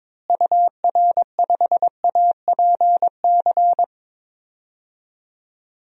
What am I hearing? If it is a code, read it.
Morse code, heard as a keyed tone, about 22 wpm, 700 Hz. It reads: UR5APC
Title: DXCC-100 Call Signs - 6 Characters - Repeated 2x - 22wpm